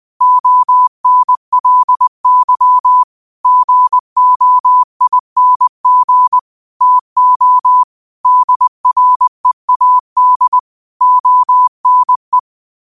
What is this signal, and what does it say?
Morse code, heard as a keyed tone, about 20 wpm, 1000 Hz. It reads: ONLY GOING TO DREAD ONE
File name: qotdfiles / 20wpm